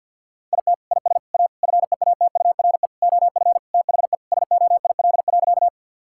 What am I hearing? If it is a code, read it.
Morse code, heard as a keyed tone, about 52 wpm, 700 Hz. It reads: IT IS A FEATURE OF THE SOIL.
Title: Sentences from Top 1000 Words - Review Part 5 - 50wpm